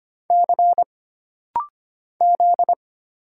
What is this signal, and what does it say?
Morse code, heard as a keyed tone, about 25 wpm, 700 Hz. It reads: C Z
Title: Single Letters - Rapid-Fire 25wpm